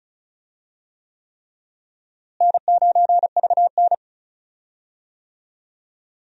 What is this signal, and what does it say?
Morse code, heard as a keyed tone, about 36 wpm, 700 Hz. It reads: N9VN
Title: USA Call Signs - 4 Characters - Repeated 3x - 35wpm